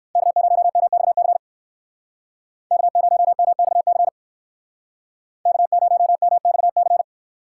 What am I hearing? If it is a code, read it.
Morse code, heard as a keyed tone, about 61 wpm, 700 Hz. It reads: K0MXC K0MXC K0MXC